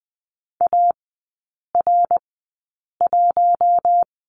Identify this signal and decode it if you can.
Morse code, heard as a keyed tone, about 20 wpm, spaced as for 8 wpm, 700 Hz. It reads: AR1
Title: 053 Lesson 10e - Letter r - Three Characters 20wpm